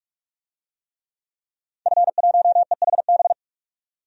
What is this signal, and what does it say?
Morse code, heard as a keyed tone, about 43 wpm, 700 Hz. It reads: F1EHB
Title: DXCC-100 Call Signs - 5 Characters - 45wpm